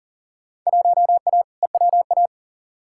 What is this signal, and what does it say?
Morse code, heard as a keyed tone, about 37 wpm, 700 Hz. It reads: 1A EWA
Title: Field Day Exchanges Encore - 40wpm